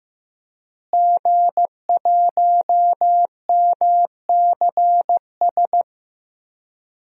Morse code, heard as a keyed tone, about 15 wpm, 700 Hz. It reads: G1MCS
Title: DXCC-100 Call Signs - 5 Characters - Repeated 3x - 15wpm